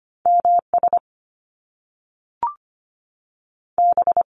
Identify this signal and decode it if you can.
Morse code, heard as a keyed tone, about 25 wpm, 700 Hz. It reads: M S B
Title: Top Two Letter Combinations in Words 25wpm